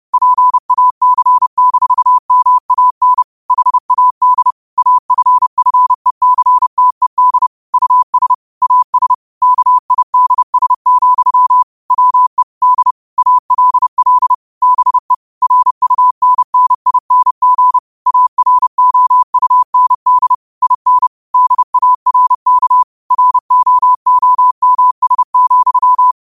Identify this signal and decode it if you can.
Morse code, heard as a keyed tone, about 30 wpm, 1000 Hz. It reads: PAC-MAN HAD AFFECTED US AS KIDS, WE D ALL BE RUNNING AROUND IN DARK ROOMS,
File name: qotdfiles / 30wpm